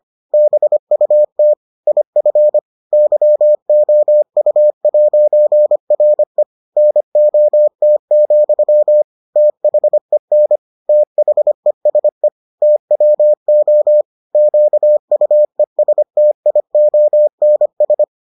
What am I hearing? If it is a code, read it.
Morse code, heard as a keyed tone, about 25 wpm, 600 Hz. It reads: BUT IF YOU'RE NOT, THEN THESE TWO QUESTIONS